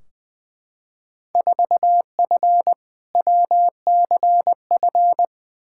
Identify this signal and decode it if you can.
Morse code, heard as a keyed tone, about 20 wpm, 700 Hz. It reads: 4F WCF
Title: Field Day Exchanges - 20wpm